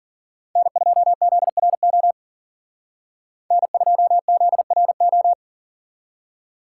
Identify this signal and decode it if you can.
Morse code, heard as a keyed tone, about 52 wpm, 700 Hz. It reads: N2ZRO N2ZRO